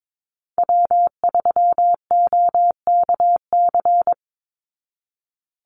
Morse code, heard as a keyed tone, about 22 wpm, 690 Hz. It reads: W3OKC
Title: Call Signs Repeated 3x - 22wpm